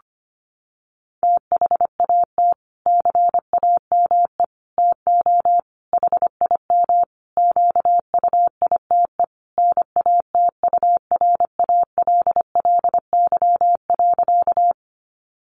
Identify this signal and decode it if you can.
Morse code, heard as a keyed tone, about 25 wpm, 700 Hz. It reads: THAT CAME TO HIM QUITE NATURALLY.
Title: Sentences from Top 1300 Words 25wpm